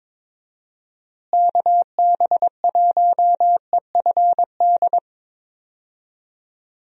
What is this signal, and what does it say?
Morse code, heard as a keyed tone, about 22 wpm, 700 Hz. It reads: KB1EFD
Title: USA Call Signs - 6 Characters - Repeated 2x - 22wpm